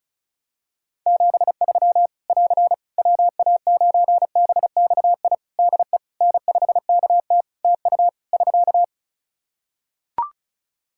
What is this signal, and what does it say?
Morse code, heard as a keyed tone, about 36 wpm, 700 Hz. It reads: 73 <AR> WA9BXI DE N5KT TU <SK>
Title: Practice QSO - Exchange 4 - Ending - 35wpm